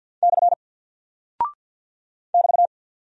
Morse code, heard as a keyed tone, about 48 wpm, 700 Hz. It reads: / <BT>
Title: Single Character (Letter:Number:Punctuation:Prosign) Alt 50wpm